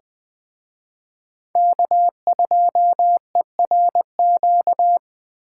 Morse code, heard as a keyed tone, about 20 wpm, 700 Hz. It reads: K2ERQ
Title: Speed Racing - USA Call Signs 17wpm